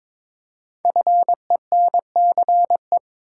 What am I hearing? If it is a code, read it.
Morse code, heard as a keyed tone, about 22 wpm, 700 Hz. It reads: FENCE
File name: 5 Letter Words - First 3 Letters Given - 22wpm